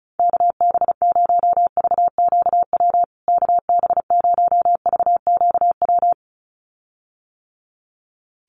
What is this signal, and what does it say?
Morse code, heard as a keyed tone, about 34 wpm, 700 Hz. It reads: KB0VQW KB0VQW
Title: USA Call Signs Repeated 2x 35wpm